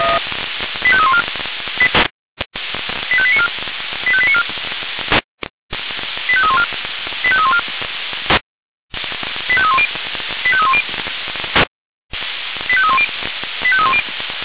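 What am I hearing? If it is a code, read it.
Five-tone selective call (ZVEI), heard as quick runs of five tones, tones of about 70 ms. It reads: 85314 85894 85894 85314 85314 85310 85310 85310 85310
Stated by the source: ZVEI five-tone call received by SDR.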